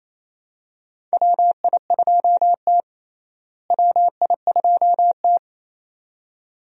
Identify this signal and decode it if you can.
Morse code, heard as a keyed tone, about 28 wpm, 700 Hz. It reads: WI2T WI2T
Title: USA Call Signs - 4 Characters - Repeated 2x - 28wpm